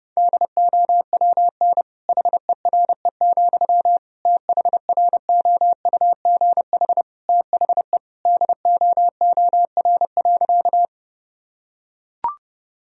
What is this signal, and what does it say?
Morse code, heard as a keyed tone, about 30 wpm, 700 Hz. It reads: DOWN HERE, THROUGH THE DOOR.
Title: Sentences from Top 1000 Words - Review Part 1 - 30wpm